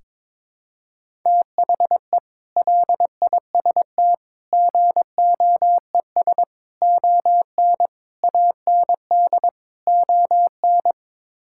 Morse code, heard as a keyed tone, about 22 wpm, 700 Hz. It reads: THE LIST GOES ON AND ON
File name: practice / 22wpm / Sets of 6 Words - 22wpm